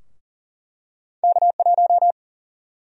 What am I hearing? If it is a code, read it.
Morse code, heard as a keyed tone, about 40 wpm, 700 Hz. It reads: K1
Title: USA Call Sign Prefixes - 40wpm